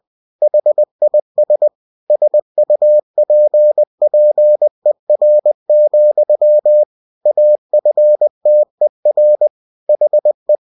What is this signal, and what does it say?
Morse code, heard as a keyed tone, about 20 wpm, 600 Hz. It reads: HIS SUPPER, AFTER HE
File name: cw-books / winnie-the-pooh / winnie-20wpm-page-0093